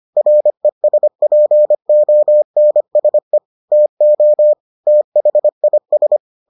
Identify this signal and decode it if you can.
Morse code, heard as a keyed tone, about 25 wpm, 600 Hz. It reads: RESPONSE TO THIS